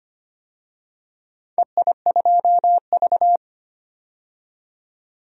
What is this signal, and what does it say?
Morse code, heard as a keyed tone, about 25 wpm, 700 Hz. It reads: EI2V
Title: DXCC-100 Call Signs - 4 Characters - 25wpm